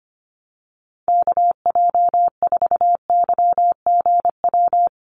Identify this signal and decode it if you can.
Morse code, heard as a keyed tone, about 25 wpm, 700 Hz. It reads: KJ4YGW
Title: USA Call Signs Repeated 3x 25wpm